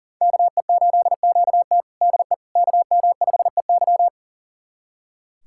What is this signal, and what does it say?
Morse code, heard as a keyed tone, about 40 wpm, 690 Hz. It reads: KE8QT DE KM5EY
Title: USA Callsign DE Callsign - 5 Characters - 40wpm